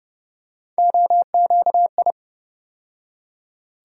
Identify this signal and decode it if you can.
Morse code, heard as a keyed tone, about 30 wpm, 700 Hz. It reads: OQI
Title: USA Call Sign Postfixes - 3 Letters 30wpm